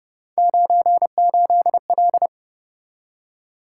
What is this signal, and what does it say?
Morse code, heard as a keyed tone, about 32 wpm, 700 Hz. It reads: 98L